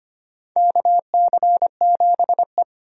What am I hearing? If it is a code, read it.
Morse code, heard as a keyed tone, about 25 wpm, 700 Hz. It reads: KC7E
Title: USA Call Signs - 4 Characters - 25wpm